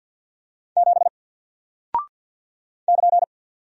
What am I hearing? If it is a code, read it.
Morse code, heard as a keyed tone, about 48 wpm, 700 Hz. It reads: 7 <KN>
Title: Single Character (Letter:Number:Punctuation:Prosign) Alt 50wpm